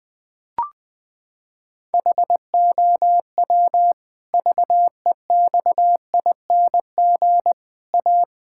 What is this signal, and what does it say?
Morse code, heard as a keyed tone, about 20 wpm, 700 Hz. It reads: HOW VEXING A
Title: Pangram Challenge v2 - 20wpm